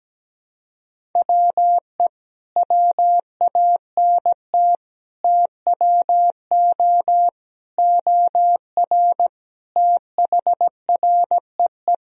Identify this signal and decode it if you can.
Morse code, heard as a keyed tone, about 17 wpm, 700 Hz. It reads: WE WANT TWO OR THREE
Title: Sentences from Top 1000 Words - Review Part 1 - 17wpm